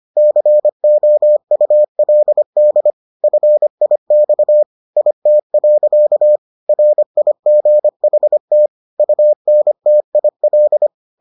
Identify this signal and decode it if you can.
Morse code, heard as a keyed tone, about 25 wpm, 600 Hz. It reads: COULD FIX IT. RIGHT UNTIL